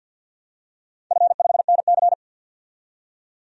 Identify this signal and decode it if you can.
Morse code, heard as a keyed tone, about 50 wpm, 700 Hz. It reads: F5NC
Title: DXCC-100 Call Signs - 4 Characters - 50wpm